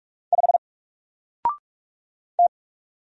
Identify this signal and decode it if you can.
Morse code, heard as a keyed tone, about 46 wpm, 700 Hz. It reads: 5 T
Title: Single Letter-Number 45wpm